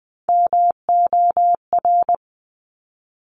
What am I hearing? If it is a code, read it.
Morse code, heard as a keyed tone, about 20 wpm, 700 Hz. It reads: MOR
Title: Top Three Letter Combinations in Words 20wpm